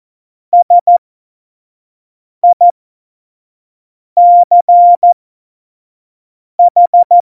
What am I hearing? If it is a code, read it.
Morse code, heard as a keyed tone, about 14 wpm, spaced as for 5 wpm, 700 Hz. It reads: SICH
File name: Morsekurs / Witze / Witz 10. September 2019, 25 F70